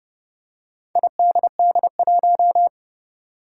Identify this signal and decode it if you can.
Morse code, heard as a keyed tone, about 31 wpm, 700 Hz. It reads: IDD1
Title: Four Challenging Letter-Number Combinations 30wpm